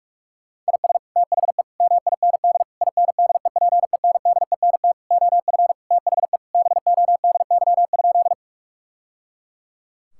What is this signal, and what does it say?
Morse code, heard as a keyed tone, about 41 wpm, 700 Hz. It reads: IS THE MIND INDEPENDENT OF THE BODY?
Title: Sentences from Top 1000 Words - Review Part 6 - 45wpm